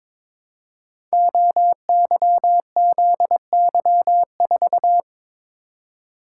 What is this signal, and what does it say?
Morse code, heard as a keyed tone, about 22 wpm, 690 Hz. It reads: OYZY4
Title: Code Groups - 5 Letter:Numbers - 22wpm